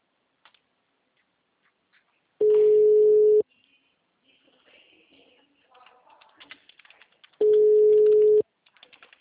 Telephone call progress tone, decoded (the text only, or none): ringback tone